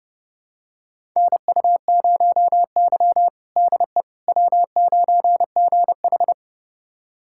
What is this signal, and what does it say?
Morse code, heard as a keyed tone, about 32 wpm, 700 Hz. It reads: NU0Y DE W9GH
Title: USA Callsign DE Callsign - 4 Characters - 30wpm